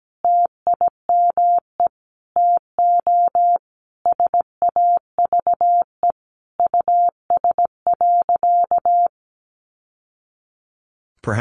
Morse code, heard as a keyed tone, about 17 wpm, 700 Hz. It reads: TIME TO SAVE US.
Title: Sentences from Top 1100 Words 17wpm